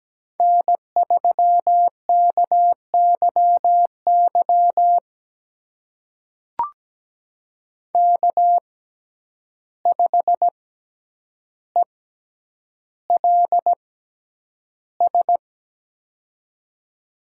Morse code, heard as a keyed tone, about 17 wpm, spaced as for 9 wpm, 700 Hz. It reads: N3KYY K5ELS
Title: USA Call Signs - 5 Characters - 6x Character Spacing - 17wpm